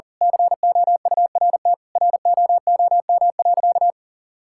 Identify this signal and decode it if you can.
Morse code, heard as a keyed tone, about 40 wpm, 700 Hz. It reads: COURT ROOM.
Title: Sentences from Top 1600 Words 40wpm